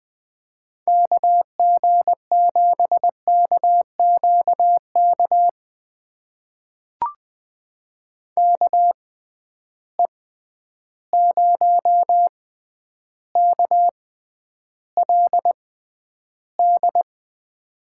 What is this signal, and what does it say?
Morse code, heard as a keyed tone, about 20 wpm, spaced as for 10 wpm, 700 Hz. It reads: KG7KQK KE0KLD
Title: USA Call Signs - 6 Characters - 6x Character Spacing - 20wpm